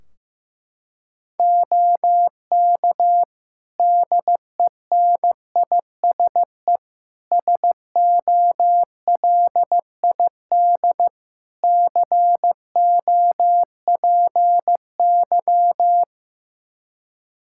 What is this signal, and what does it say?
Morse code, heard as a keyed tone, about 15 wpm, 700 Hz. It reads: OK DENISE SOLID COPY
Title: Practice QSO - Exchange 3 - More Info - 15wpm